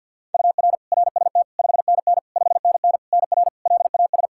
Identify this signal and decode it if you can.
Morse code, heard as a keyed tone, about 51 wpm, 700 Hz. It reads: UR RST 5NN 5NN NR LAS
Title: K1USN SST QSO Practice 50wpm